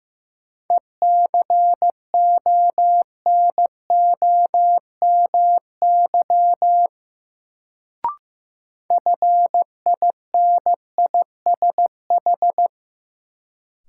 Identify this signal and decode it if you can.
Morse code, heard as a keyed tone, about 15 wpm, 700 Hz. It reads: ECONOMY FINISH
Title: Sentences from Top 1000 Words - Review Part 3 - New Word Drill - 15wpm